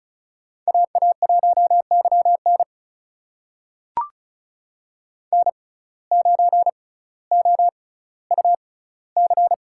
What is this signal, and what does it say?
Morse code, heard as a keyed tone, about 36 wpm, spaced as for 18 wpm, 700 Hz. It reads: AA1YN N9OUC
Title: USA Call Signs - 5 Characters - 6x Character Spacing - 35wpm